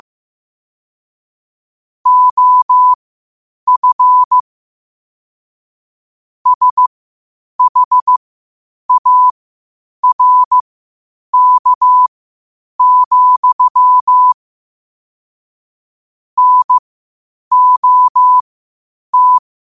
Morse code, heard as a keyed tone, about 15 wpm, spaced as for 8 wpm, 1000 Hz. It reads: OF SHARK, NOT
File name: qotdfiles / 8wpm